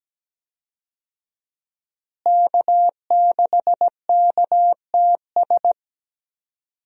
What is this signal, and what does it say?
Morse code, heard as a keyed tone, about 17 wpm, 700 Hz. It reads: K6KTS
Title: USA Call Signs - 5 Characters - Repeated 2x - 17wpm